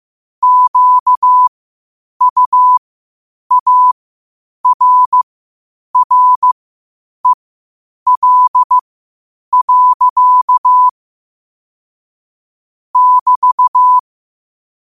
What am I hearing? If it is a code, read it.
Morse code, heard as a keyed tone, about 15 wpm, spaced as for 8 wpm, 1000 Hz. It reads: QUARREL. <BT>
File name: qotdfiles / 8wpm